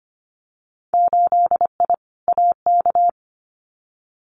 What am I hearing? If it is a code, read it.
Morse code, heard as a keyed tone, about 25 wpm, 700 Hz. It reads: 8I AK
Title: Winter Field Day Exchanges Encore 25wpm